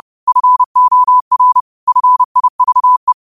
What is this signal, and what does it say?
Morse code, heard as a keyed tone, about 30 wpm, 1000 Hz. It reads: FOR FIVE
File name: qotdfiles / 30wpm